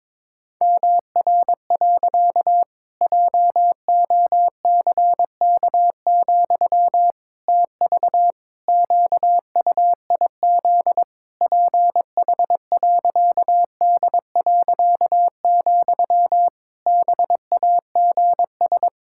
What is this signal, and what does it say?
Morse code, heard as a keyed tone, about 22 wpm, 700 Hz. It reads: MR. JOCK, TV QUIZ PH.D., BAGS
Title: Pangram Challenge v2 - 22wpm